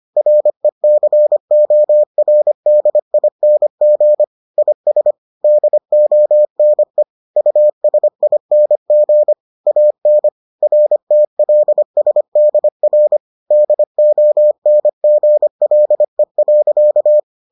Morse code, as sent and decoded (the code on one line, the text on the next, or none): .-. -.-. --- .-. -.. .. -. --. / .. ... / -.. --- -. / ..- ... .. -. --. / .- -. / .-. - .-.. ... -.. .-. / -.. --- -. --. .-.. .-.-.-
RECORDING IS DONE USING AN RTLSDR DONGLE.